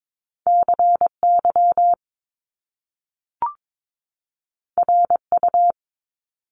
Morse code, heard as a keyed tone, about 22 wpm, 700 Hz. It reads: CY RU
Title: Top Two Letter Combinations in Words 22wpm